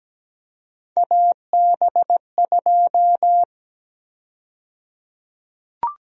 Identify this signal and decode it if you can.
Morse code, heard as a keyed tone, about 17 wpm, 700 Hz. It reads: AB2
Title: USA Call Sign Prefixes - Repeated 2x - 17wpm